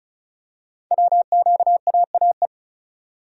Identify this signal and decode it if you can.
Morse code, heard as a keyed tone, about 36 wpm, 700 Hz. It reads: WQAAE
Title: Code Groups - 5 Letter:Numbers - 35wpm